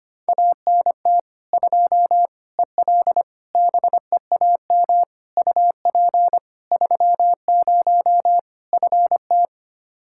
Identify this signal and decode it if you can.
Morse code, heard as a keyed tone, about 25 wpm, 700 Hz. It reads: ANT 2 EL BEAM UP 30 FT
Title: K1USN SST QSO Practice 25wpm